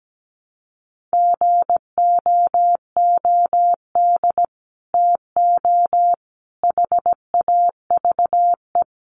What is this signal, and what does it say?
Morse code, heard as a keyed tone, about 17 wpm, 700 Hz. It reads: GOOD TO HAVE
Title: Sets of 5 Words Encore2 17wpm